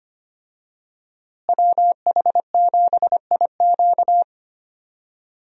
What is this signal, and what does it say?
Morse code, heard as a keyed tone, about 25 wpm, 700 Hz. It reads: WH7IQ
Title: USA Call Signs - 5 Characters - Repeated 3x - 25wpm